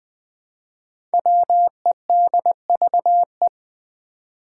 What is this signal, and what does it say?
Morse code, heard as a keyed tone, about 20 wpm, 700 Hz. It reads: WEDVE
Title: Five Challenging Letter-Number Combinations 20wpm